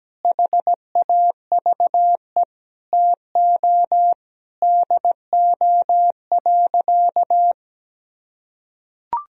Morse code, heard as a keyed tone, about 17 wpm, 700 Hz. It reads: HAVE TO DO.
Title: Sentences from Top 1000 Words - Review Part 3 - 17wpm